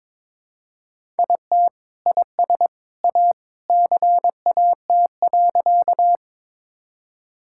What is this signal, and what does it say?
Morse code, heard as a keyed tone, about 22 wpm, 700 Hz. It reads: IT IS A CAT.
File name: practice / 22wpm / Sentences with 2-Word Phrases - 22wpm